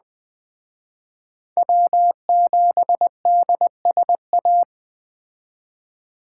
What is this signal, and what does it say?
Morse code, heard as a keyed tone, about 21 wpm, 700 Hz. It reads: W7DSA